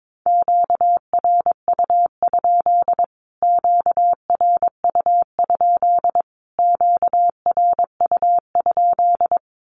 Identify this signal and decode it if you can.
Morse code, heard as a keyed tone, about 22 wpm, 700 Hz. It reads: QRU? QRU? QRU?
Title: Q-Codes Repeated 3x 22wpm